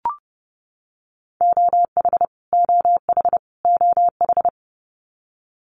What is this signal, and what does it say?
Morse code, heard as a keyed tone, about 28 wpm, 700 Hz. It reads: OH OH OH
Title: US States Repeated 3x 30wpm